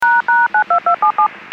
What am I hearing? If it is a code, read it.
DTMF, heard as telephone touch tones, tones of about 110 ms. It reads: ##92277
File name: WPS 451.25